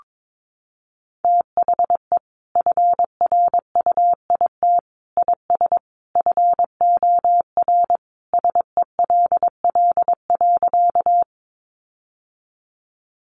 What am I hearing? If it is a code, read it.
Morse code, heard as a keyed tone, about 22 wpm, 700 Hz. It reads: THE FRUIT IS FOR SELL.
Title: Sentences from Top 1300 Words 22wpm